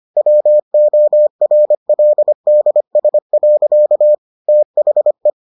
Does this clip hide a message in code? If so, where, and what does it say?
Yes: Morse code from 0.2 to 5.3 s, heard as a keyed tone, about 25 wpm, 600 Hz. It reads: WORLDS. THE